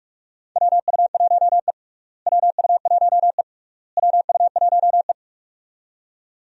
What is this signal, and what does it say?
Morse code, heard as a keyed tone, about 46 wpm, 700 Hz. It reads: WU1E WU1E WU1E
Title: USA Call Signs - 4 Characters - Repeated 3x - 45wpm